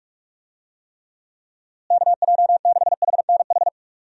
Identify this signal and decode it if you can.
Morse code, heard as a keyed tone, about 41 wpm, 700 Hz. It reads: KJ6HNH
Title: USA Call Signs - 6 Characters - Repeated 3x - 45wpm